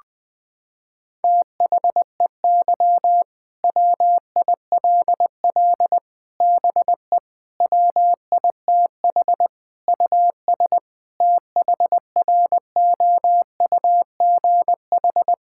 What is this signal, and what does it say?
Morse code, heard as a keyed tone, about 20 wpm, 700 Hz. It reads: THEY WILL BE WITH US THROUGH
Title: Sentences from Top 1600 Words 20wpm